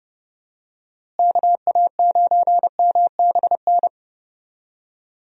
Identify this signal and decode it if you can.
Morse code, heard as a keyed tone, about 30 wpm, 700 Hz. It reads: KA9MBN
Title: USA Call Signs - 6 Characters - Repeated 2x - 30wpm